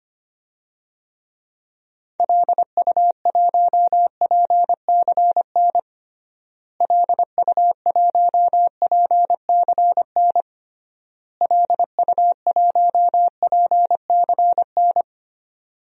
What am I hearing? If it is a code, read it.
Morse code, heard as a keyed tone, about 25 wpm, 700 Hz. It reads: LU1PCN LU1PCN LU1PCN
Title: DXCC-100 Call Signs - 6 Characters - Repeated 3x - 25wpm